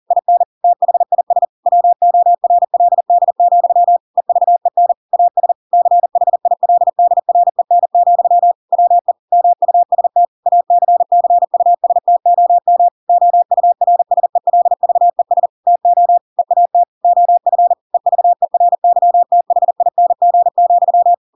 Morse code, heard as a keyed tone, about 40 wpm, 700 Hz. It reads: IN THIS WORLD, EVEN AS CHILDREN, WE MUST ACCUSTOM OURSELVES TO EAT OF EVERYTHING,
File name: Pinocchio / Pinocchio ch7 40 40 ik1hge